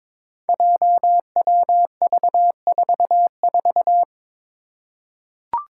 Morse code, heard as a keyed tone, about 22 wpm, 700 Hz. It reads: JWV44
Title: Five Challenging Letter-Number Combinations 22wpm